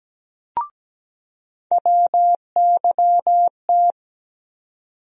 Morse code, heard as a keyed tone, about 17 wpm, 700 Hz. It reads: WYT
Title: USA Call Sign Postfixes - 3 Letters 17wpm